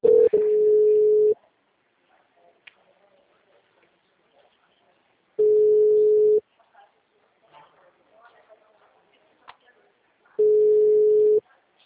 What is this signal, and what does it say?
Ringback tone (Europe, 425 Hz), with speech.
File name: out- -2025- - - .19275